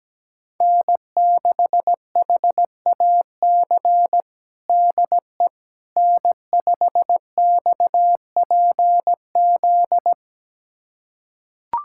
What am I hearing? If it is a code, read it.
Morse code, heard as a keyed tone, about 17 wpm, 700 Hz. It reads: N6HAC DE N5XPZ
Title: USA Callsign DE Callsign - 5 Characters - 17wpm